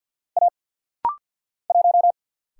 Morse code, heard as a keyed tone, about 50 wpm, 690 Hz. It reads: A 1
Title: Single Letter-Number Alt - Rapid-Fire 50wpm